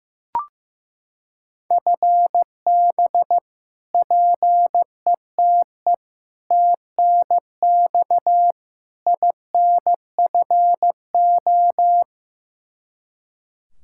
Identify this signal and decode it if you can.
Morse code, heard as a keyed tone, about 15 wpm, 700 Hz. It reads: FB PETE TNX INFO
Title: Practice QSO - Exchange 3 - More Info - 15wpm